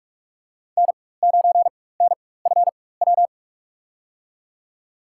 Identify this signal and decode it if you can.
Morse code, heard as a keyed tone, about 46 wpm, spaced as for 21 wpm, 700 Hz. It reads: N9NFW
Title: USA Call Signs - 4x Character Spacing - 45wpm